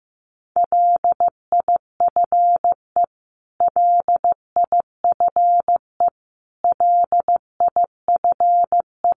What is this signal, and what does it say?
Morse code, heard as a keyed tone, about 15 wpm, 700 Hz. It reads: LIFE LIFE LIFE
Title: Top 500 Words Repeated 15wpm